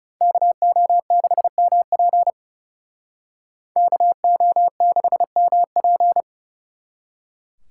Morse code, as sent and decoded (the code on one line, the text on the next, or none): -.- --- -.... -- .--. / -.- --- -.... -- .--.
KO6MP KO6MP